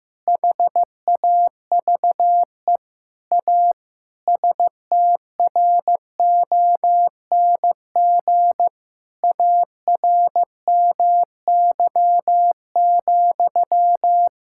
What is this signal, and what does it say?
Morse code, heard as a keyed tone, about 15 wpm, 700 Hz. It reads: HAVE A STRONG ARMY,
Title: Sentences from Top 1000 Words - Review Part 5 - 15wpm